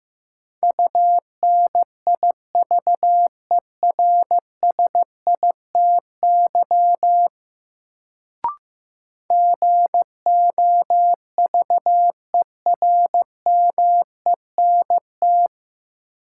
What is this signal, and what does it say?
Morse code, heard as a keyed tone, about 15 wpm, 700 Hz. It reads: UNIVERSITY GOVERMENT
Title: Sentences from Top 1300 Words - New Word Drill Alt - 15wpm